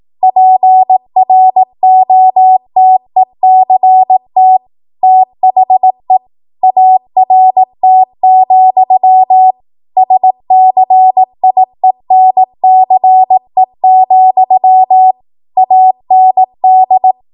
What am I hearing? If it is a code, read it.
Morse code, heard as a keyed tone, about 18 wpm, 750 Hz. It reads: PROTECT THE ART, SCIENCE, AND
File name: Morse / 18WPM